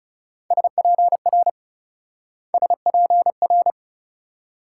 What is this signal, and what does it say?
Morse code, heard as a keyed tone, about 37 wpm, 700 Hz. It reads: SPR SPR
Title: Speed-Racing Multi-letter Phonemes - 30wpm